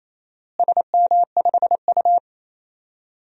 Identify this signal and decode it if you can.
Morse code, heard as a keyed tone, about 28 wpm, 700 Hz. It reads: SM5U
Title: DXCC-100 Call Signs - 4 Characters - Repeated 3x - 28wpm